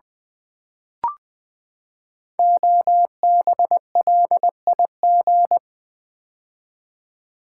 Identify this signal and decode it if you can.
Morse code, heard as a keyed tone, about 20 wpm, 700 Hz. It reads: OBLIG
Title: Build A Word - 5 Letters At A Time - 10-Letter Words - 20wpm